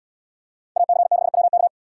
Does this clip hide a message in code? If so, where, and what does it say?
Yes: Morse code from 0.8 to 1.7 s, heard as a keyed tone, about 72 wpm, 700 Hz. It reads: AC6WK